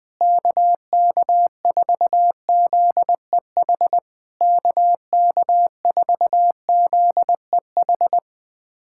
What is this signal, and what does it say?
Morse code, heard as a keyed tone, about 21 wpm, 700 Hz. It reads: KK4ZEH KK4ZEH
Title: SOTA QSO Practice 20wpm